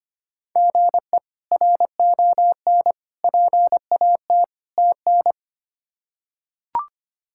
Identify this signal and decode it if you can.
Morse code, heard as a keyed tone, about 25 wpm, 700 Hz. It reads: GE RON PAT TN
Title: K1USN SST QSO Practice - 25wpm